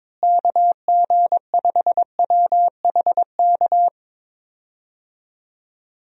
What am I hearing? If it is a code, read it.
Morse code, heard as a keyed tone, about 23 wpm, 700 Hz. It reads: KG5WHK
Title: USA Call Signs - 6 Characters - Repeated 2x - 22wpm